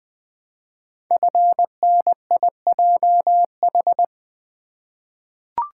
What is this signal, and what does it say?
Morse code, heard as a keyed tone, about 20 wpm, 700 Hz. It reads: FNIJH
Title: Code Groups - 5 Letters - 20wpm